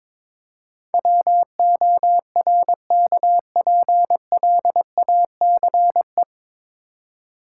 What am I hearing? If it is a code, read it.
Morse code, heard as a keyed tone, about 23 wpm, 700 Hz. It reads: WORKPLACE
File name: 9 Letter Words - First 3 Letters Given - 22wpm